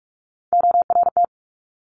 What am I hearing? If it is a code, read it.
Morse code, heard as a keyed tone, about 46 wpm, 700 Hz. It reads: ORT
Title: Top Three Letter Combinations in Words 45wpm